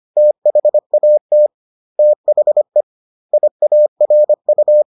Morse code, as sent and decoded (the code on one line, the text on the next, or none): - .... .- - / - .... / .. .- .-. ..-
THAT THE IARU